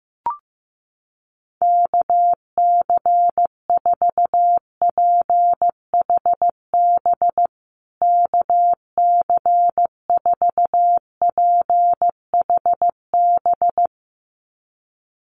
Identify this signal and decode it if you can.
Morse code, heard as a keyed tone, about 15 wpm, 700 Hz. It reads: KC4PHB KC4PHB
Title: USA Call Signs Repeated 2x 15wpm